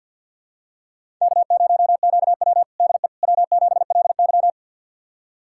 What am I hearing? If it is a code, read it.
Morse code, heard as a keyed tone, about 52 wpm, 700 Hz. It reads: K0QW DE W7LY
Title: USA Callsign DE Callsign - 4 Characters - 50wpm